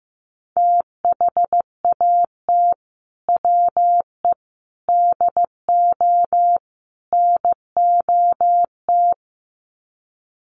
Morse code, heard as a keyed tone, about 15 wpm, 700 Hz. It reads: THAT WE DO NOT
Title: Sets of 4 Words 15wpm